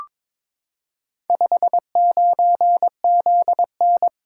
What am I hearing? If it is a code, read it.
Morse code, heard as a keyed tone, about 22 wpm, 700 Hz. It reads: 59ZN
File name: practice / 22wpm / Four Letter-Number Combinations - 22wpm